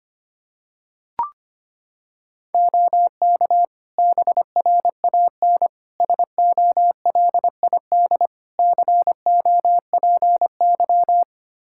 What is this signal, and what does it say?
Morse code, heard as a keyed tone, about 25 wpm, 700 Hz. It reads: OK BRAN SOLID COPY
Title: Practice QSO - Exchange 3 - More Info - 25wpm